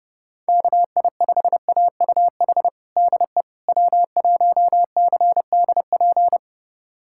Morse code, heard as a keyed tone, about 32 wpm, 700 Hz. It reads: KI5AUH DE W1CDP
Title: USA Callsign DE Callsign 30wpm